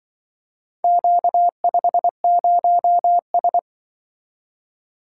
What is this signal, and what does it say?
Morse code, heard as a keyed tone, about 23 wpm, 700 Hz. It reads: Q50S
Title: Speed-Racing - Four Letter-Number Combinations - 20wpm